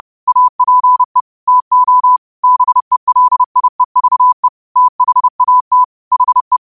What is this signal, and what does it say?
Morse code, heard as a keyed tone, about 30 wpm, 1000 Hz. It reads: APE TO BELIEVE THAT HE
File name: qotdfiles / 30wpm